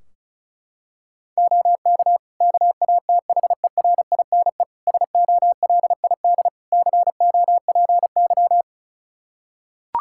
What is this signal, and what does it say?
Morse code, heard as a keyed tone, about 36 wpm, 700 Hz. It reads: OK KATHERINE SOLID COPY
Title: Practice QSO - Exchange 3 - More Info - 35wpm